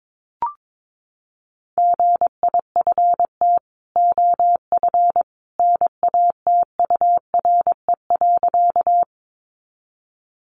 Morse code, heard as a keyed tone, about 22 wpm, 700 Hz. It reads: GIFT OF NATURE.
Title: Sentences from Top 900 Words 22wpm